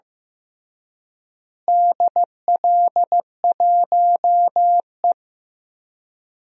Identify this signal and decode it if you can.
Morse code, heard as a keyed tone, about 15 wpm, 700 Hz. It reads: DL1E
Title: DXCC-100 Call Signs - 4 Characters - Repeated 2x - 15wpm